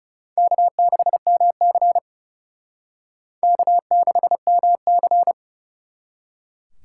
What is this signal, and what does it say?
Morse code, heard as a keyed tone, about 38 wpm, 700 Hz. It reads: K6MC K6MC